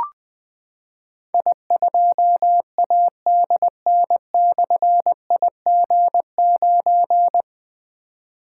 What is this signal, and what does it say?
Morse code, heard as a keyed tone, about 20 wpm, 700 Hz. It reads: I2ADN/IG9
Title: Difficult Call Signs 20wpm